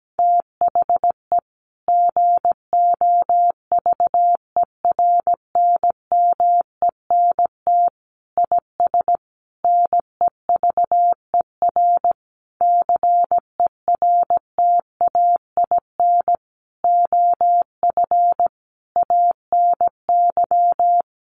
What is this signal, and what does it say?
Morse code, heard as a keyed tone, about 17 wpm, 700 Hz. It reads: THE GOVERNMENT IS NEVER CERTAIN OF ANY
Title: Sentences from Top 1300 Words 17wpm